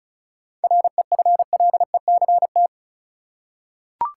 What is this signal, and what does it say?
Morse code, heard as a keyed tone, about 34 wpm, 700 Hz. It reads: REFLECT
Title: Seven Letter Words - 35wpm